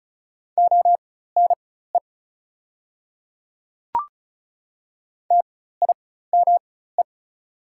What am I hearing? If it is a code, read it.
Morse code, heard as a keyed tone, about 35 wpm, spaced as for 16 wpm, 690 Hz. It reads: ONE TIME
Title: 4x Character Spacing - Top 100 Words - 35wpm